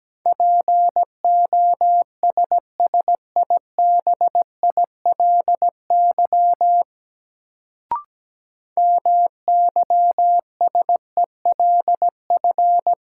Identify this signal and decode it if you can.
Morse code, heard as a keyed tone, about 17 wpm, 700 Hz. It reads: POSSIBILY MYSELF
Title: Sentences from Top 1300 Words - New Word Drill Alt - 17wpm